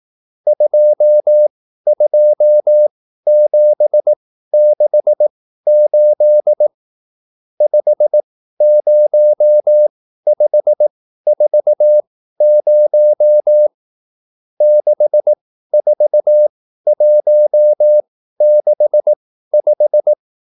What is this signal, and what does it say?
Morse code, heard as a keyed tone, about 18 wpm, spaced as for 13 wpm, 600 Hz. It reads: 22768 50540 64165